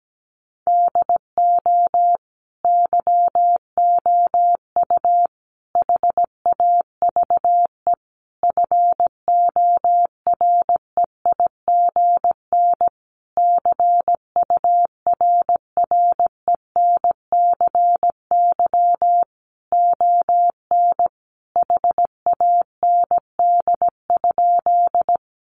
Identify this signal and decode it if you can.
Morse code, heard as a keyed tone, about 17 wpm, 700 Hz. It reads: DO YOU HAVE FOREIGN CURRENCY ON HAND?
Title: Sentences from Top 1500 Words 17wpm